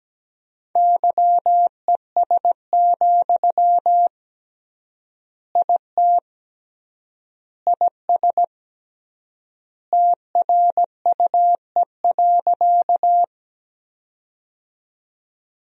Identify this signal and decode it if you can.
Morse code, heard as a keyed tone, about 17 wpm, 700 Hz. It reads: YES, IT IS TRUE.
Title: 3x Word-Spacing - Sentences from Top 500 Words - 17wpm